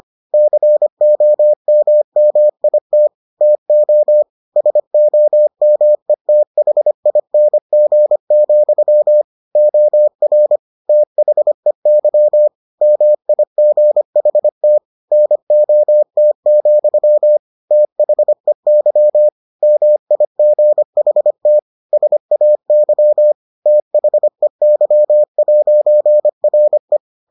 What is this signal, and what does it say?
Morse code, heard as a keyed tone, about 25 wpm, 600 Hz. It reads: COMMIT TO SOMETHING, OR THEY MIGHT NOT, THEY MIGHT SAY THEY'RE